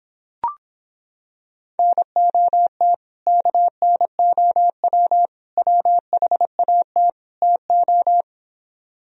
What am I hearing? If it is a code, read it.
Morse code, heard as a keyed tone, about 26 wpm, 700 Hz. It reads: NOT KNOW WHAT TO